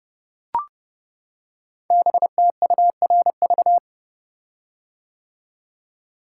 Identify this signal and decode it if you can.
Morse code, heard as a keyed tone, about 30 wpm, 700 Hz. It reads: BTURV
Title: Code Groups - 5 Letters - 30wpm